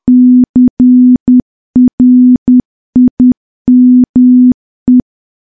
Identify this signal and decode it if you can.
Morse code, heard as a keyed tone, about 10 wpm, 250 Hz. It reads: CRIME
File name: morse / morse-training-10WPM